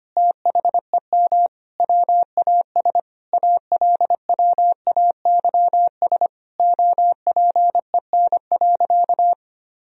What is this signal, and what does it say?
Morse code, heard as a keyed tone, about 25 wpm, 700 Hz. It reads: THEM WAS ALWAYS OPEN.
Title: Sentences from Top 1000 Words - Review Part 1 - 25wpm